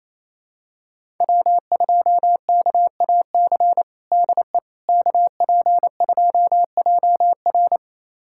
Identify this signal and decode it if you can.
Morse code, heard as a keyed tone, about 28 wpm, 700 Hz. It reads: W2KAC DE KP2JR
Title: USA Callsign DE Callsign - 5 Characters - 28wpm